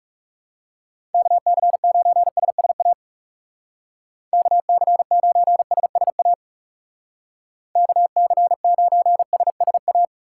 Morse code, heard as a keyed tone, about 35 wpm, 700 Hz. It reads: KC9SSA KC9SSA KC9SSA
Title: Speed Racing - USA Call Signs 30wpm